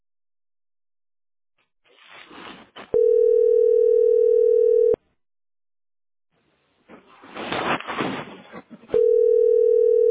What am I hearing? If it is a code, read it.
Ringback tone (North America), 440 + 480 Hz.